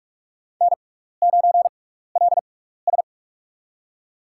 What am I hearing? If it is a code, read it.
Morse code, heard as a keyed tone, about 45 wpm, spaced as for 15 wpm, 700 Hz. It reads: N9LS
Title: USA Call Signs - 6x Character Spacing - 45wpm